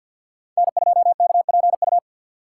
Morse code, heard as a keyed tone, about 52 wpm, 700 Hz. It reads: N2KPU
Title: USA Call Signs - 2x Character Spacing - 50wpm